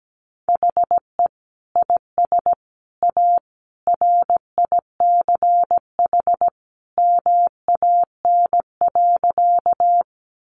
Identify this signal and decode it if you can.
Morse code, heard as a keyed tone, about 17 wpm, 700 Hz. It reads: HE IS A RICH MAN.
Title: Sentences from Top 1100 Words 17wpm